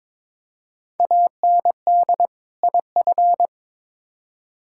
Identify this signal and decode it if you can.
Morse code, heard as a keyed tone, about 22 wpm, 700 Hz. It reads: AND IF
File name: practice / 22wpm / Build A Phrase - 5-Word Phrases - 22wpm